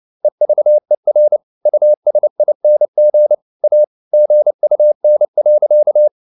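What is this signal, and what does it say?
Morse code, heard as a keyed tone, about 29 wpm, 600 Hz. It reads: EVER USING A GUN.
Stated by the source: Morse code at 29 wpm, sending Ken Follett's novel Never in English, from CW med Gnister.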